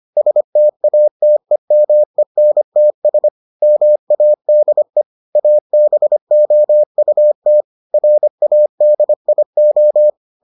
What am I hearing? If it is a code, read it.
Morse code, heard as a keyed tone, about 25 wpm, 600 Hz. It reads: STATEMENTS MADE ABOUT RADIO